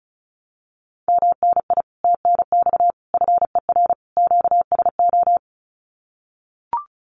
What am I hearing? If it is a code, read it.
Morse code, heard as a keyed tone, about 36 wpm, 700 Hz. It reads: MNI TNX FER QSO
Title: Common QSO Phrases Encore 35wpm